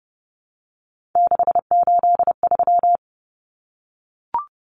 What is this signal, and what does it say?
Morse code, heard as a keyed tone, about 30 wpm, 700 Hz. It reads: 683
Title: Numbers - 3-Digit 30wpm